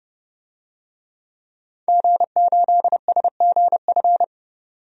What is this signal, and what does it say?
Morse code, heard as a keyed tone, about 30 wpm, 690 Hz. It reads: G8SGF
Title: DXCC-100 Call Signs - 5 Characters - 30wpm